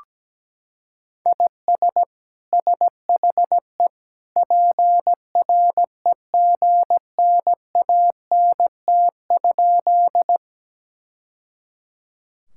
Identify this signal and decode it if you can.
Morse code, heard as a keyed tone, about 17 wpm, 700 Hz. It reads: IS SHE PREGNANT?
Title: Sentences from Top 1000 Words - Review Part 6 - 17wpm